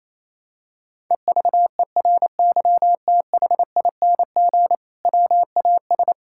Morse code, heard as a keyed tone, about 28 wpm, 700 Hz. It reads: EVERYTHING WAS
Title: Sentences from Top 1600 Words 28wpm